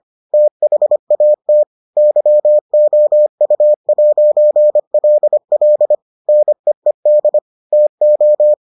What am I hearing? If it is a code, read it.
Morse code, heard as a keyed tone, about 25 wpm, 600 Hz. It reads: THAT YOU'LL NEED TO